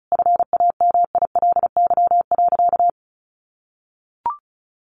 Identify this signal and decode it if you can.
Morse code, heard as a keyed tone, about 34 wpm, 700 Hz. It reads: FAMILY.
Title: Sentences from Top 1500 Words 35wpm